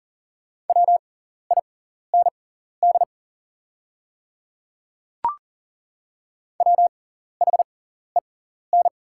Morse code, heard as a keyed tone, about 40 wpm, spaced as for 14 wpm, 690 Hz. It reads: WIND WHEN
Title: 6x Character Spacing - Top 500 Words - 40wpm